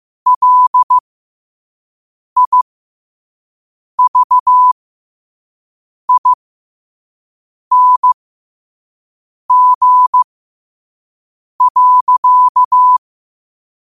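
Morse code, heard as a keyed tone, about 15 wpm, spaced as for 5 wpm, 1000 Hz. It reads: LIVING.
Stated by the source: Morse code sent at 5 words per minute.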